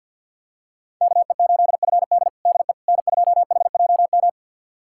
Morse code, heard as a keyed tone, about 50 wpm, 690 Hz. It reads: KE8FD DE N2HJM
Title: USA Callsign DE Callsign - 5 Characters - 50wpm